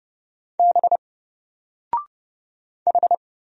Morse code, heard as a keyed tone, about 31 wpm, 700 Hz. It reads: B H
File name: Single-Letter-Number-30wpm